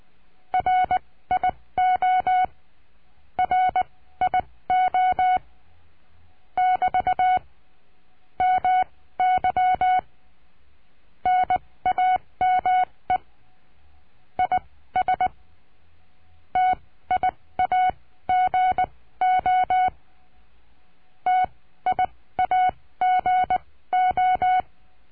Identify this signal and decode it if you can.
Morse code, heard as a keyed tone, about 20 wpm, spaced as for 15 wpm, 740 Hz. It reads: RIO RIO <BT> MY NAME IS TIAGO TIAGO
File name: cursos / CURSO DE TELEGRAFIA / Telegrafia - Trilha 22